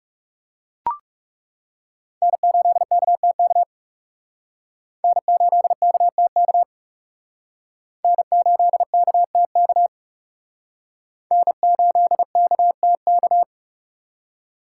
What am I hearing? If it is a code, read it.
Morse code, heard as a keyed tone, about 37 wpm, 700 Hz. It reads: N8KTK N8KTK N8KTK N8KTK
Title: Speed Racing - USA Call Signs 30wpm